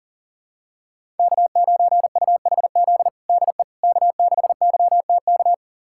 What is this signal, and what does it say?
Morse code, heard as a keyed tone, about 40 wpm, 700 Hz. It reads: K9UHZ DE K6YTK
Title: USA Callsign DE Callsign - 5 Characters - 40wpm